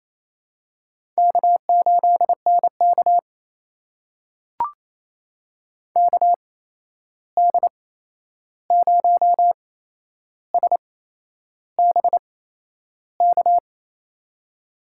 Morse code, heard as a keyed tone, about 28 wpm, spaced as for 11 wpm, 700 Hz. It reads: K8NK KD0SBK
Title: USA Call Signs - 8x Character Spacing - 28wpm